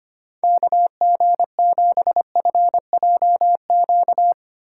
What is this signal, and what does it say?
Morse code, heard as a keyed tone, about 31 wpm, 700 Hz. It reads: KG7FJQ
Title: Speed Racing - USA Call Signs 25wpm